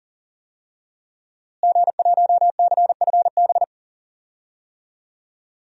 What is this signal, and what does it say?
Morse code, heard as a keyed tone, about 40 wpm, 700 Hz. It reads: G1CFB
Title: DXCC-100 Call Signs - 5 Characters - Repeated 2x - 40wpm